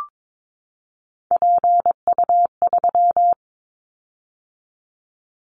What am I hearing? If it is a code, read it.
Morse code, heard as a keyed tone, about 22 wpm, 700 Hz. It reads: PU3
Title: Callsign Prefixes for Top 100 DXCC - 22wpm